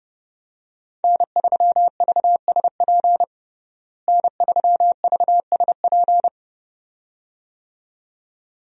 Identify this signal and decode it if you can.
Morse code, heard as a keyed tone, about 30 wpm, 700 Hz. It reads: N3VSP N3VSP
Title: USA Call Signs - 5 Characters - Repeated 2x - 30wpm